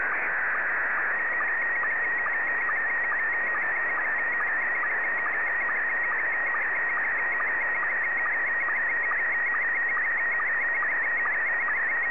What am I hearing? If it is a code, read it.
SSTV (slow-scan television), Scottie 1.